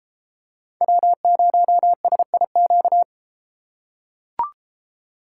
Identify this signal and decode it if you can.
Morse code, heard as a keyed tone, about 31 wpm, 700 Hz. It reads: W0SIQ